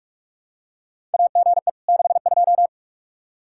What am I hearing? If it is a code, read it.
Morse code, heard as a keyed tone, about 43 wpm, 700 Hz. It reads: AGE 62
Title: Practice QSO - Exchange 3 - More Info - 45wpm